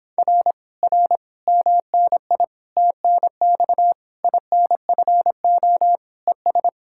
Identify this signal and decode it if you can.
Morse code, heard as a keyed tone, about 26 wpm, 700 Hz. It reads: R R MNI TNX INFO ES